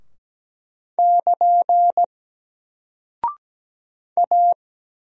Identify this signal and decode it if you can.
Morse code, heard as a keyed tone, about 17 wpm, 700 Hz. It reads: <KN> A
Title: Single Character (Letter:Number:Punctuation:Prosign) Alt 17wpm